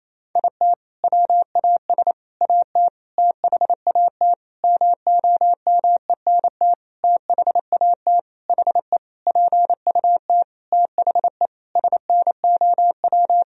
Morse code, heard as a keyed tone, about 28 wpm, 700 Hz. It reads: IT WAS AT THAT MOMENT THAT HE PUT THE SNOW
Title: Sentences from Top 1000 Words - Review Part 3 - 28wpm